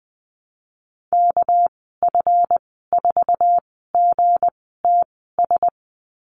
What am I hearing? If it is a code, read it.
Morse code, heard as a keyed tone, about 20 wpm, spaced as for 14 wpm, 700 Hz. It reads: KF4GTS
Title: 245 Lesson 39g - Letter z - Call Signs 20wpm